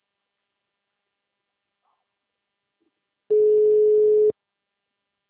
Ringback tone (Europe, 425 Hz).